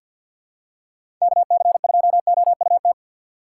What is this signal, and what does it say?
Morse code, heard as a keyed tone, about 52 wpm, 700 Hz. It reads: KC2OUT
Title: USA Call Signs - 6 Characters - 6x Character Spacing - 50wpm